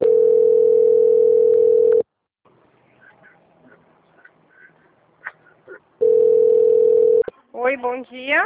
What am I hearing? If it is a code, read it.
Ringback tone (North America), 440 + 480 Hz, with speech.
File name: out- - - - - .4429